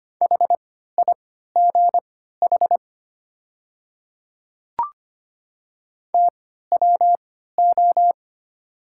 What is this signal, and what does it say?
Morse code, heard as a keyed tone, about 25 wpm, spaced as for 14 wpm, 700 Hz. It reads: HIGH TWO
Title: 3x Character Spacing - Top 300 Words - 25wpm